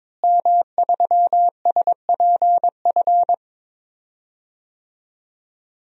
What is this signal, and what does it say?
Morse code, heard as a keyed tone, about 22 wpm, 700 Hz. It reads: M3SPF
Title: DXCC-100 Call Signs - 5 Characters - Repeated 3x - 22wpm